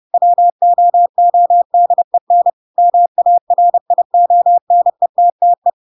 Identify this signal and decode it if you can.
Morse code, heard as a keyed tone, about 30 wpm, 700 Hz. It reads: WOODEN MARIONETTE
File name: Pinocchio / Pinocchio ch3 30 30 ik1hge